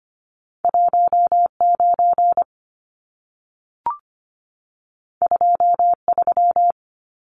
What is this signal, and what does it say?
Morse code, heard as a keyed tone, about 25 wpm, 700 Hz. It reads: 19 23
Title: Numbers - 2-Digit 25wpm